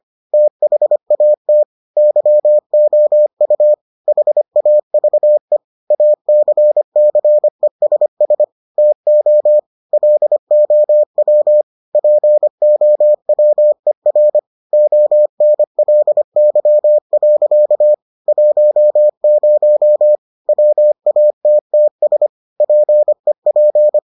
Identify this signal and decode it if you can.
Morse code, heard as a keyed tone, about 25 wpm, 600 Hz. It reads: THAT YOU HAVE ACCESS TO LOW POWER ONLY. 10 WATTS PEP